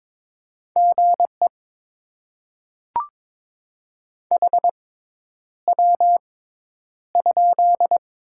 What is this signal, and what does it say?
Morse code, heard as a keyed tone, about 22 wpm, spaced as for 10 wpm, 700 Hz. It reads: GE HW?
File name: Top 20 QSO Elements - 6x Character Spacing - 22wpm